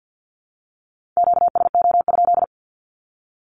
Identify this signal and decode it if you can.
Morse code, heard as a keyed tone, about 48 wpm, 700 Hz. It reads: QSO?
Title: Q-Codes 50wpm